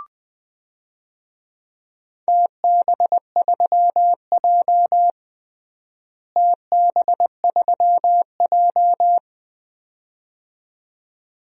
Morse code, heard as a keyed tone, about 20 wpm, 700 Hz. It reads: TB3J TB3J
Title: DXCC-100 Call Signs - 4 Characters - Repeated 2x - 20wpm